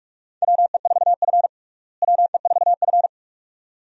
Morse code, heard as a keyed tone, about 46 wpm, 700 Hz. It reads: WE4F WE4F
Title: USA Call Signs - 4 Characters - Repeated 2x - 45wpm